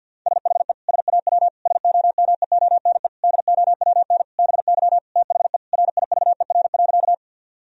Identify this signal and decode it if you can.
Morse code, heard as a keyed tone, about 50 wpm, 700 Hz. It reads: SHE SAW SOMEONE DOWN BY THE RIVER.
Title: Sentences from Top 1000 Words - Review Part 4 - 50wpm